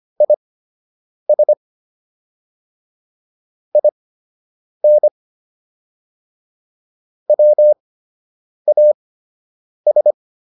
Morse code, heard as a keyed tone, about 25 wpm, spaced as for 8 wpm, 600 Hz. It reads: IS IN WAS